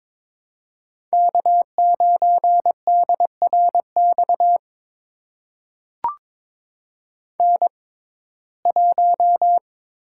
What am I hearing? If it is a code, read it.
Morse code, heard as a keyed tone, about 22 wpm, spaced as for 16 wpm, 700 Hz. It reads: K9DRX N1
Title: USA Call Signs - 5 Characters - 6x Character Spacing - 22wpm